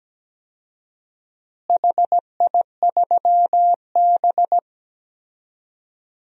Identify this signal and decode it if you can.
Morse code, heard as a keyed tone, about 17 wpm, 700 Hz. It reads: HI3B
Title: DXCC-100 Call Signs - 4 Characters - Repeated 2x - 17wpm